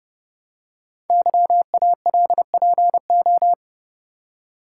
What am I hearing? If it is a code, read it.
Morse code, heard as a keyed tone, about 30 wpm, 700 Hz. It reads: YALPO
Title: Code Groups - 5 Letters - 30wpm